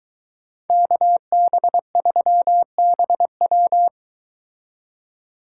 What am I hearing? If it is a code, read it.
Morse code, heard as a keyed tone, about 23 wpm, 700 Hz. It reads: KB3BW